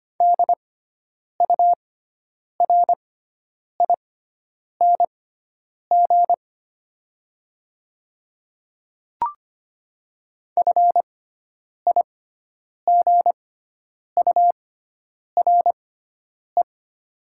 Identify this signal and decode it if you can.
Morse code, heard as a keyed tone, about 25 wpm, spaced as for 9 wpm, 700 Hz. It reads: DURING FIGURE
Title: 6x Character Spacing - Top 500 Words - 25wpm